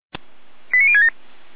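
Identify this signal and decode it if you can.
Five-tone selective call (ZVEI), heard as quick runs of five tones, tones of about 70 ms. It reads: 89067
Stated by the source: ZVEI five-tone selective call received on SDR.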